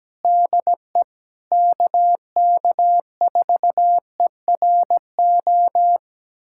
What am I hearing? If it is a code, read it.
Morse code, heard as a keyed tone, about 17 wpm, 700 Hz. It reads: DE KK4ERO
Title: USA Callsign DE Callsign 17wpm